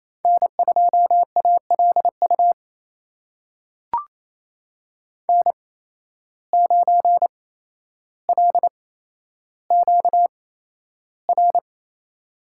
Morse code, heard as a keyed tone, about 28 wpm, spaced as for 12 wpm, 700 Hz. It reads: N2ALU N9LQR
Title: USA Call Signs - 5 Characters - 8x Character Spacing - 28wpm